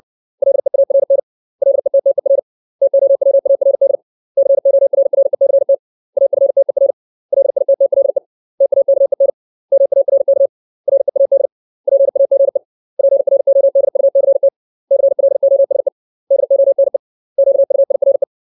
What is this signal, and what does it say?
Morse code, heard as a keyed tone, about 60 wpm, 550 Hz. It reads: LEARN LETTER TOWARD CORRECT AFTER LITTLE TAKEN MARK READ PAGE PRODUCT CLOSE DONE QUIRE
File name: 632 words / 632 Words 60WPM